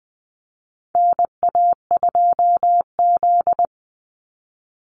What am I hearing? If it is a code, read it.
Morse code, heard as a keyed tone, about 20 wpm, 700 Hz. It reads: NA2Z
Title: USA Call Signs 20wpm